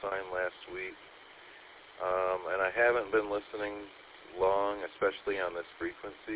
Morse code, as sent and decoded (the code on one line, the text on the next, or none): none